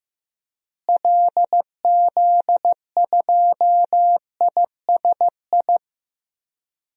Morse code, heard as a keyed tone, about 15 wpm, 700 Hz. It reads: LZ2ISI
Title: DXCC-100 Call Signs - 6 Characters - Repeated 3x - 15wpm